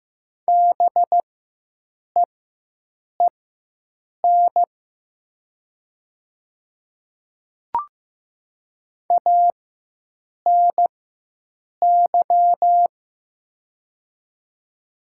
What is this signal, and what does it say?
Morse code, heard as a keyed tone, about 15 wpm, spaced as for 7 wpm, 700 Hz. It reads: BEEN ANY
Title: 4x Character Spacing - Top 500 Words - 15wpm